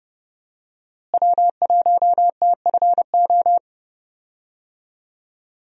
Morse code, heard as a keyed tone, about 31 wpm, 700 Hz. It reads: W1TFO
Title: USA Call Signs - 5 Characters - Repeated 2x - 30wpm